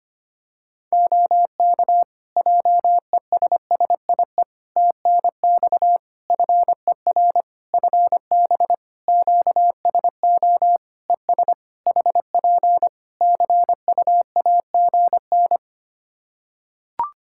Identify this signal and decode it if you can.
Morse code, heard as a keyed tone, about 25 wpm, 700 Hz. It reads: OK JESSIE TNX FER FB QSO ES HP CUAGN
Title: Practice QSO - Exchange 4 - Ending - 25wpm